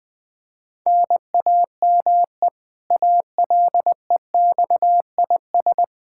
Morse code, heard as a keyed tone, about 20 wpm, 700 Hz. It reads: NAME ALEXIS
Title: Practice QSO - Exchange 1 - Beginning - 20wpm